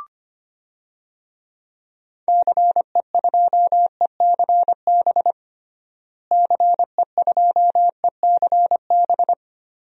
Morse code, heard as a keyed tone, about 25 wpm, 700 Hz. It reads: CE2ECB CE2ECB
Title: DXCC-100 Call Signs - 6 Characters - Repeated 3x - 25wpm